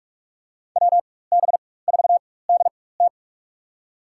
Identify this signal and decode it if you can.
Morse code, heard as a keyed tone, about 42 wpm, spaced as for 20 wpm, 700 Hz. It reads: WB4DT
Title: USA Call Signs - 5 Characters - 4x Character Spacing - 45wpm